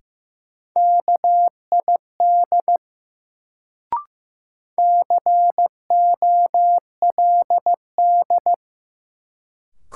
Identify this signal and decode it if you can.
Morse code, heard as a keyed tone, about 16 wpm, 700 Hz. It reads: KID COLD
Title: Sets of 5 Words - New Word Drill - 15wpm